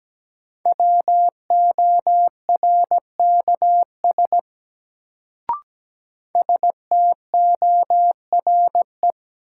Morse code, heard as a keyed tone, about 17 wpm, 700 Hz. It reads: WORKS STORE
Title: Sentences from Top 1000 Words - Review Part 5 - New Word Drill - 17wpm